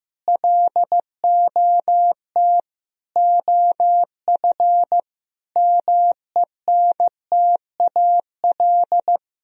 Morse code, heard as a keyed tone, about 15 wpm, 700 Hz. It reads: LOT OF MENTAL
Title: Sentences from Top 1000 Words - Review Part 4 - 15wpm